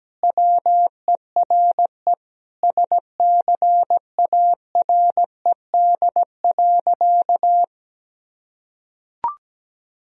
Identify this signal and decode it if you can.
Morse code, heard as a keyed tone, about 17 wpm, 700 Hz. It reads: WERE SCARED.
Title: Sentences from Top 1000 Words - Review Part 4 - 17wpm